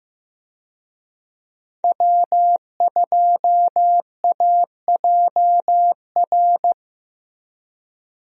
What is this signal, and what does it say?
Morse code, heard as a keyed tone, about 15 wpm, 700 Hz. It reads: W2AJR
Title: USA Call Signs - 5 Characters - Repeated 3x - 15wpm